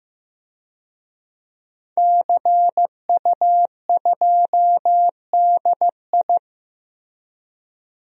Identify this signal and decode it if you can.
Morse code, heard as a keyed tone, about 15 wpm, 700 Hz. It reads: CU2DI
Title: DXCC-100 Call Signs - 5 Characters - Repeated 2x - 15wpm